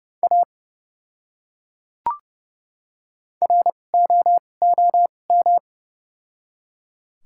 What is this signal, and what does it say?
Morse code, heard as a keyed tone, about 30 wpm, spaced as for 22 wpm, 700 Hz. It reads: A ROOM
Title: 2x Character Spacing - Top 500 Words - 30wpm